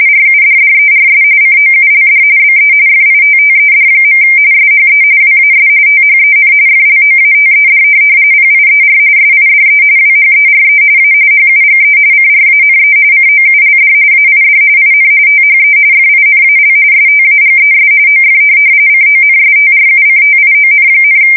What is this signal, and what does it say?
RTTY, about 45.45 baud. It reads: RUDDERS COULD BE FITTED. SHE WAS THEN FLOATED PIER SIDE WHERE A HUGE CRANE COULD AID IN THE FINAL STAGES OF CONSTRUCTION,
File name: fsk170 / USS-New-Jersey